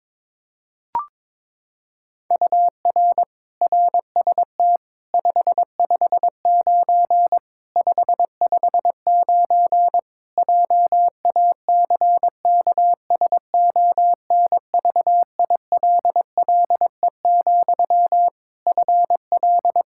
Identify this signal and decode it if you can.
Morse code, heard as a keyed tone, about 22 wpm, 700 Hz. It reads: UR RST 559 559 JACKSONVILLE, FL
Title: Practice QSO - Exchange 1 - Beginning - 22wpm